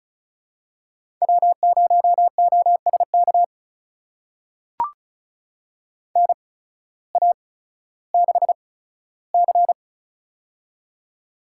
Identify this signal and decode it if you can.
Morse code, heard as a keyed tone, about 36 wpm, spaced as for 16 wpm, 700 Hz. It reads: W0OSK NA6C
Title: USA Call Signs - 8x Character Spacing - 35wpm